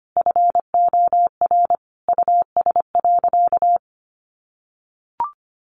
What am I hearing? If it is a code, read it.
Morse code, heard as a keyed tone, about 25 wpm, 700 Hz. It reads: FOR US.
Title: Sentences from Top 1300 Words 25wpm